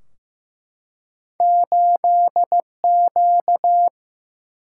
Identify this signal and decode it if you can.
Morse code, heard as a keyed tone, about 15 wpm, 700 Hz. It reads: 8Q
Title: Single-Letter Single-Number 15wpm